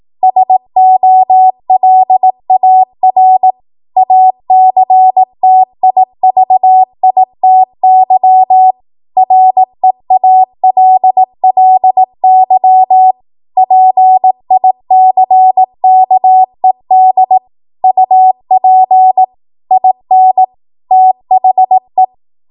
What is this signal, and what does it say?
Morse code, heard as a keyed tone, about 18 wpm, 750 Hz. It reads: SOLAR ACTIVITY REALLY PICKED UP IN THE
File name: Archive / 18 WPM / 18WPM